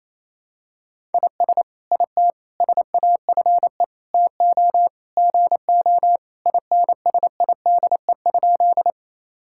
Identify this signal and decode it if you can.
Morse code, heard as a keyed tone, about 28 wpm, 700 Hz. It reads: IS IT SAFE TO GO INSIDE?
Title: Sentences from Top 1000 Words - Review Part 6 - 28wpm